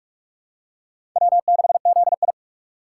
Morse code, heard as a keyed tone, about 46 wpm, 700 Hz. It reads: W6ZI
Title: USA Call Signs - 4 Characters - 3x Character Spacing - 45wpm